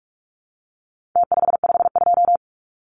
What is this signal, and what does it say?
Morse code, heard as a keyed tone, about 44 wpm, 700 Hz. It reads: T552
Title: Numbers - Contest Style - 4-Digit 45wpm